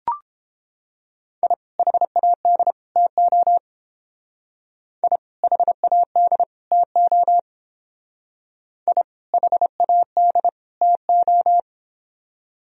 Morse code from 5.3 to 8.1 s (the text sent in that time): HAD TO